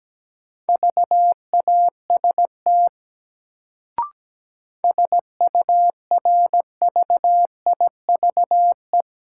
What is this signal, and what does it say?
Morse code, heard as a keyed tone, about 17 wpm, 700 Hz. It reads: VAST SURVIVE
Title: Sentences from Top 1300 Words - New Word Drill Alt - 17wpm